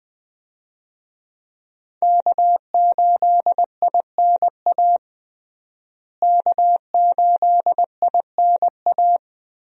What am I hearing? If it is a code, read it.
Morse code, heard as a keyed tone, about 20 wpm, 700 Hz. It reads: K8INA K8INA
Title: USA Call Signs - 5 Characters - Repeated 2x - 20wpm